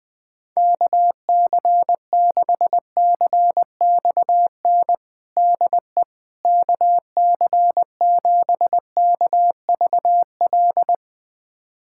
Morse code, heard as a keyed tone, about 20 wpm, 700 Hz. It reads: KC6CXN DE KC7KVL
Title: USA Callsign DE Callsign - 6 Characters - 20wpm